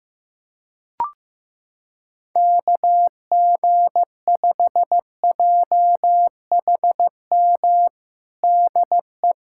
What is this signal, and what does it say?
Morse code, heard as a keyed tone, about 15 wpm, 700 Hz. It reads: KG5JHM DE
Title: USA Callsign DE Callsign - 6 Characters - 15wpm